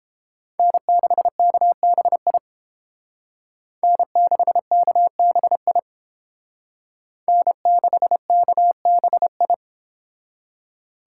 Morse code, heard as a keyed tone, about 30 wpm, 700 Hz. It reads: N6KBI N6KBI N6KBI